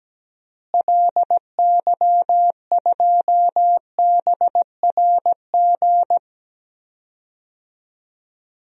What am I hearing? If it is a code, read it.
Morse code, heard as a keyed tone, about 17 wpm, 700 Hz. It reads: LY2BRG
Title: DXCC-100 Call Signs - 6 Characters - Repeated 3x - 17wpm